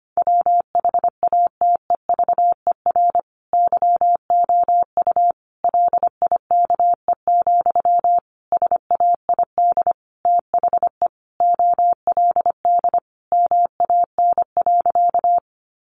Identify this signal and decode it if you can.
Morse code, heard as a keyed tone, about 25 wpm, 700 Hz. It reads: WHATEVER YOU LIKE, SAID THE OLD MAN.
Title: Sentences from Top 1100 Words 25wpm